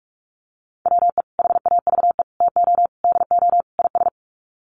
Morse code, heard as a keyed tone, about 44 wpm, 700 Hz. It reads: WE HAVE TO DO IS
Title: Sets of 5 Words Encore 45wpm